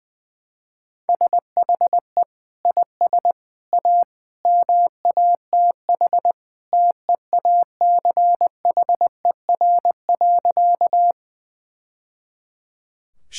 Morse code, heard as a keyed tone, about 20 wpm, 700 Hz. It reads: SHE IS A MATH TEACHER.
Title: Sentences from Top 1000 Words - Review Part 6 - 20wpm